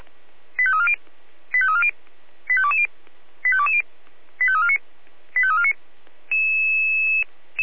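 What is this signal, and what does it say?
Five-tone selective call (ZVEI), heard as quick runs of five tones, tones of about 70 ms. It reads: 85340 85340 85220 85220 85349 85349
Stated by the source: ZVEI five-tone selective call received by SDR.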